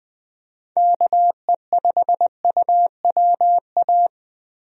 Morse code, heard as a keyed tone, about 20 wpm, 700 Hz. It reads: KE5UWA
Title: USA Call Signs - 6 Characters - 20wpm